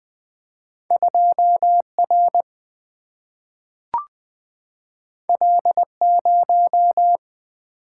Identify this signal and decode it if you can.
Morse code, heard as a keyed tone, about 20 wpm, 700 Hz. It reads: 2 R L 0
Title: Single-Letter Single-Number 20wpm